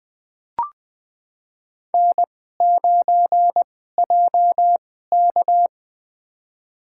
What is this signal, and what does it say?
Morse code, heard as a keyed tone, about 20 wpm, spaced as for 14 wpm, 700 Hz. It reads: N9JK
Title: USA Call Signs - 4 Characters - 2x Character Spacing - 20wpm